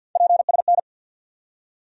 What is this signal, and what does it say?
Morse code, heard as a keyed tone, about 47 wpm, 700 Hz. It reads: PSN
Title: USA Call Sign Postfixes - 3 Letters 50wpm